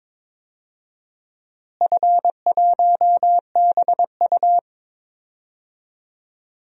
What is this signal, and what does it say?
Morse code, heard as a keyed tone, about 22 wpm, 700 Hz. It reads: F1BU
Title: DXCC-100 Call Signs - 4 Characters - Repeated 3x - 22wpm